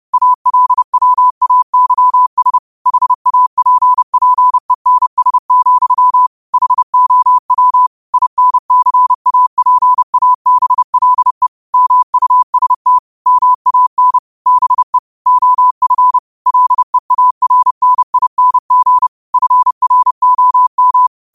Morse code, heard as a keyed tone, about 30 wpm, 1000 Hz. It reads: ALWAYS HAPPENS, HOW INCAPABLE MUST MAN BE OF LEARNING FROM
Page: Quotes for Mon, 28 Apr 2025 in Morse Code at 30 words per minute.